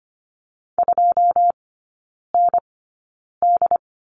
Morse code, heard as a keyed tone, about 25 wpm, spaced as for 9 wpm, 700 Hz. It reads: 2ND
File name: Lesson 15e - Number 5 - Three Characters 25wpm